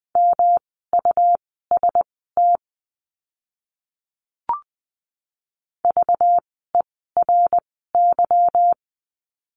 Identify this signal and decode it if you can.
Morse code, heard as a keyed tone, about 20 wpm, spaced as for 14 wpm, 700 Hz. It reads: MUST VERY
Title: 237 Lesson 38f - Letter q - Words 20wpm